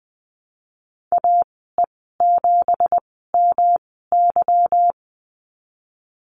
Morse code, heard as a keyed tone, about 20 wpm, spaced as for 14 wpm, 700 Hz. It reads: AE7MY
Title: 167 Lesson 27g - Number 7 - Call Signs 20wpm